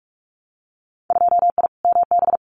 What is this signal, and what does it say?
Morse code, heard as a keyed tone, about 46 wpm, 700 Hz. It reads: 2I MB
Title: Winter Field Day Exchanges Encore 45wpm